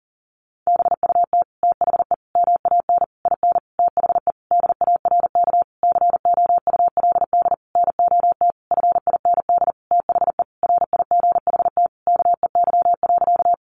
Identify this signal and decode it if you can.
Morse code, heard as a keyed tone, about 37 wpm, 700 Hz. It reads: BUT THE MAN IN THE DARK COULD NOT FIND THE RIGHT KEY.
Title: Sentences from Top 600 Words 40wpm